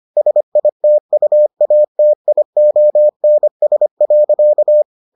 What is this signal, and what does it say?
Morse code, heard as a keyed tone, about 25 wpm, 600 Hz. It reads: SITUATIONS.